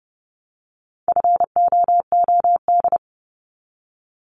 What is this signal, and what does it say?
Morse code, heard as a keyed tone, about 30 wpm, 700 Hz. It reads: FOOD
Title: Multi-letter Phonemes 30wpm